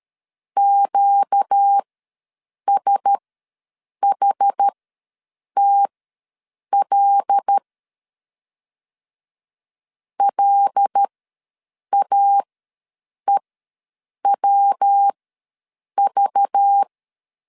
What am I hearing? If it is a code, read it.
Morse code, heard as a keyed tone, about 13 wpm, spaced as for 7 wpm, 800 Hz. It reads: QSHTL LAEWV
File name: LEZIONI / 5 LBVC / 5 LBV ws 12wpm 3015